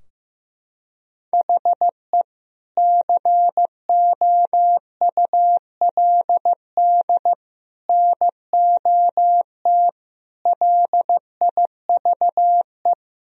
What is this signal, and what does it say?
Morse code, heard as a keyed tone, about 15 wpm, 700 Hz. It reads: HE COULD NOT LIVE
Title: Sentences from Top 1000 Words - Review Part 3 - 15wpm